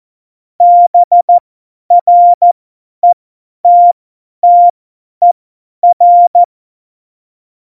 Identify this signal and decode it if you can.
Morse code, heard as a keyed tone, about 14 wpm, spaced as for 10 wpm, 700 Hz. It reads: BRETTER
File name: Morsekurs / Witze / Witz 22. Mai 2018, 50 F70